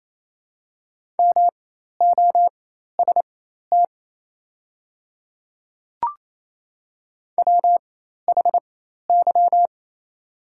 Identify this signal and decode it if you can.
Morse code, heard as a keyed tone, about 27 wpm, spaced as for 13 wpm, 690 Hz. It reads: MOST WHY
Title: 4x Character Spacing - Top 200 Words - 28wpm